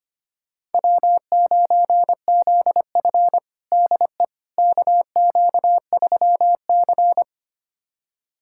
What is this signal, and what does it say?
Morse code, heard as a keyed tone, about 25 wpm, 700 Hz. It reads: W9ZF DE KQ3C
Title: USA Callsign DE Callsign - 4 Characters - 25wpm